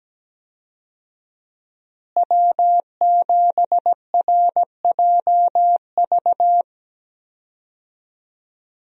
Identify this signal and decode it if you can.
Morse code, heard as a keyed tone, about 17 wpm, 700 Hz. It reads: W7RJV
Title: USA Call Signs - 5 Characters - Repeated 3x - 17wpm